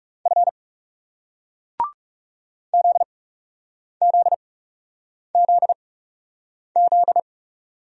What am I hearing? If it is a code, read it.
Morse code, heard as a keyed tone, about 40 wpm, spaced as for 8 wpm, 700 Hz. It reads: F ZZZZ